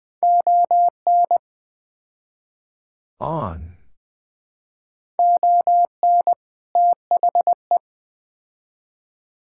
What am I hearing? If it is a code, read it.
Morse code, heard as a keyed tone, about 20 wpm, 700 Hz. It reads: ON ON THE
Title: Build A Phrase - 4-Word Phrases - 20wpm